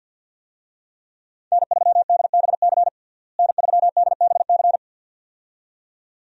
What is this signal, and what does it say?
Morse code, heard as a keyed tone, about 48 wpm, 700 Hz. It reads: N3DBC N3DBC
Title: USA Call Signs - 5 Characters - Repeated 2x - 50wpm